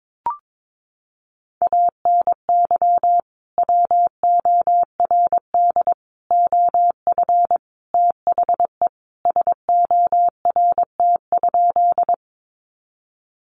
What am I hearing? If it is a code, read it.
Morse code, heard as a keyed tone, about 22 wpm, 700 Hz. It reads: ANY WORD OF THE SORT?
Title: Sentences from Top 1300 Words 22wpm